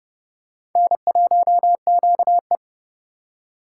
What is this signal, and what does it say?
Morse code, heard as a keyed tone, about 27 wpm, 700 Hz. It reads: N1QE